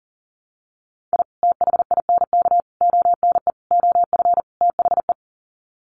Morse code, heard as a keyed tone, about 40 wpm, 700 Hz. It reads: I THINK ONE OF THE
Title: Sets of 5 Words 40wpm